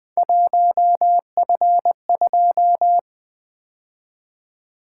Morse code, heard as a keyed tone, about 20 wpm, 700 Hz. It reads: 1F2
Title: Three Letter-Number Combinations - 20wpm